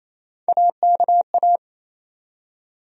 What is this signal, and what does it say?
Morse code, heard as a keyed tone, about 28 wpm, 700 Hz. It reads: AKA
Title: USA Call Sign Postfixes - 3 Letters 28wpm